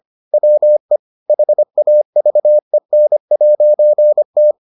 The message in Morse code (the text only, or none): WE HAVEN'T